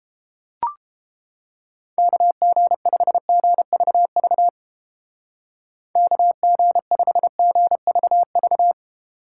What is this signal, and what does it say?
Morse code, heard as a keyed tone, about 30 wpm, 700 Hz. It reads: KG5GVV KG5GVV